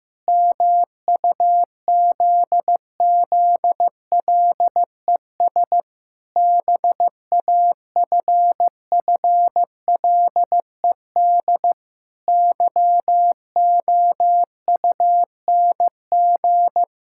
Morse code, heard as a keyed tone, about 15 wpm, 700 Hz. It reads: MUZZLES BAFFLED YOUNG
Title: Pangram Challenge v2 - 15wpm